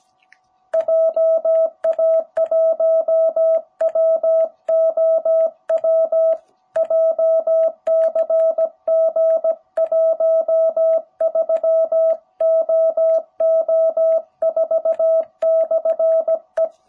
Morse code, heard as a keyed tone, about 17 wpm, 660 Hz. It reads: JA1WOW JCG 13OO4/E
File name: jh8cbh / 2021cw